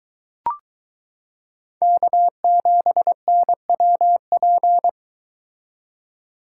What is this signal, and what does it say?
Morse code, heard as a keyed tone, about 22 wpm, 700 Hz. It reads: K7NWP